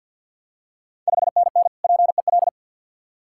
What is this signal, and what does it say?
Morse code, heard as a keyed tone, about 53 wpm, 690 Hz. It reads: 5NN PEL
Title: Ontario QSO Party Exchange 50wpm